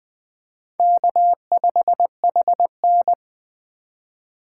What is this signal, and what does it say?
Morse code, heard as a keyed tone, about 20 wpm, 700 Hz. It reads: K5HN
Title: Speed Racing - USA 1x2 Call Signs 17wpm